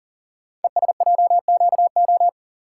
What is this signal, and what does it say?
Morse code, heard as a keyed tone, about 42 wpm, 700 Hz. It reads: ESJQO
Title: Code Groups - 5 Letters - 40wpm